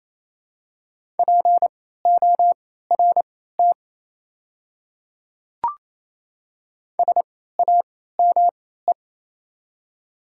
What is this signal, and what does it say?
Morse code, heard as a keyed tone, about 28 wpm, spaced as for 16 wpm, 700 Hz. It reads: PORT SAME
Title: 3x Character Spacing - Top 200 Words - 28wpm